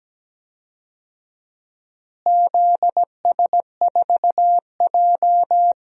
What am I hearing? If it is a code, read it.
Morse code, heard as a keyed tone, about 17 wpm, 700 Hz. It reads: ZS4J
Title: DXCC-100 Call Signs - 4 Characters - Repeated 3x - 17wpm